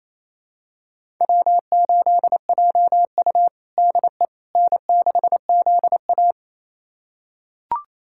Morse code, heard as a keyed tone, about 29 wpm, 700 Hz. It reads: W8JU DE N6ZA
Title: USA Callsign DE Callsign - 4 Characters - 28wpm